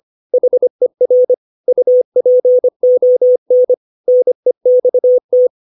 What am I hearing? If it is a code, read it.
Morse code, heard as a keyed tone, about 25 wpm, 500 Hz. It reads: HER UPON NEXT
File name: 632 words / 632 Words 25WPM